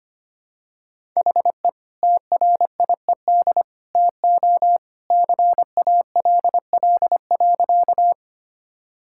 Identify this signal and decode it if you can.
Morse code, heard as a keyed tone, about 25 wpm, 700 Hz. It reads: HE TRIED TO CALL.
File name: practice / 25wpm / Sentences with 2-Word Phrases - 25wpm